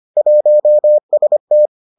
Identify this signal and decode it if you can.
Morse code, heard as a keyed tone, about 25 wpm, 600 Hz. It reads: 1ST